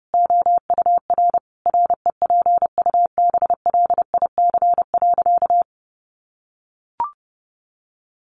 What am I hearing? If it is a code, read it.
Morse code, heard as a keyed tone, about 30 wpm, 700 Hz. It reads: OUR REPUBLIC.
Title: Sentences from Top 1500 Words 30wpm